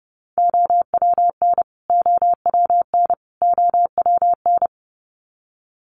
Morse code, heard as a keyed tone, about 30 wpm, 700 Hz. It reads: OWN OWN OWN
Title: Top 300 Words Repeated 30wpm